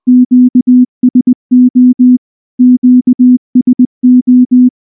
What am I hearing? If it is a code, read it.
Morse code, heard as a keyed tone, about 19 wpm, 250 Hz. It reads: QSO QSO
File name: morse / morse-training-20WPM-4